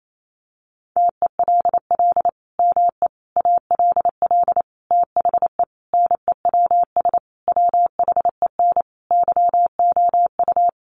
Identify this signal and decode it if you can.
Morse code, heard as a keyed tone, about 28 wpm, 700 Hz. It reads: TELL ME ALL THE NEWS WHEN YOU
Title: Sentences from Top 800 Words 28wpm